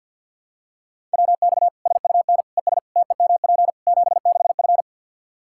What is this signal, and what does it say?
Morse code, heard as a keyed tone, about 51 wpm, 700 Hz. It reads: WX SUN ES TEMP 76F
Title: K1USN SST QSO Practice 50wpm